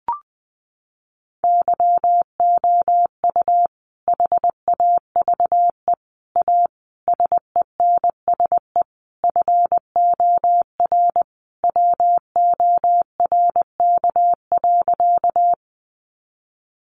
Morse code, heard as a keyed tone, about 20 wpm, 700 Hz. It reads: YOU HAVE A SENSE FOR WORK.
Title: Sentences from Top 700 Words 20wpm